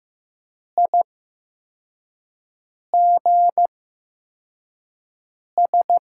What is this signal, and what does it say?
Morse code, heard as a keyed tone, about 15 wpm, spaced as for 4 wpm, 700 Hz. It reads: IGS
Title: USA Call Signs - 6 Characters - 8x Character Spacing - 15wpm